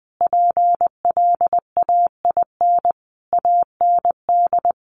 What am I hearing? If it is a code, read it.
Morse code, heard as a keyed tone, about 20 wpm, 700 Hz. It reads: PLAIN AND
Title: Sentences from Top 1400 Words 20wpm